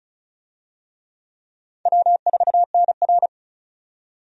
Morse code, heard as a keyed tone, about 36 wpm, 700 Hz. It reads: W4NR
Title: USA Call Signs - 4 Characters - 35wpm